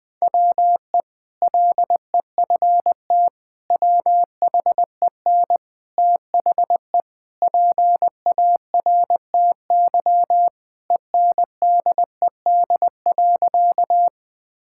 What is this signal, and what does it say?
Morse code, heard as a keyed tone, about 20 wpm, 700 Hz. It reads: WE LEFT WHEN THE PARTY ENDED.
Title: Sentences with 2-Word Phrases - 20wpm